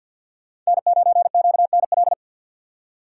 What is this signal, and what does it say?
Morse code, heard as a keyed tone, about 62 wpm, 700 Hz. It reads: N9QNL
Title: Speed Racing - USA Call Signs 50wpm